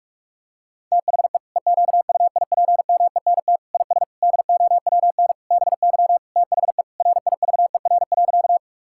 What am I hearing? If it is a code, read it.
Morse code, heard as a keyed tone, about 46 wpm, 690 Hz. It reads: THE EQUIPMENT IS DOWN BY THE RIVER.
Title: Sentences from Top 1000 Words - Review Part 5 - 45wpm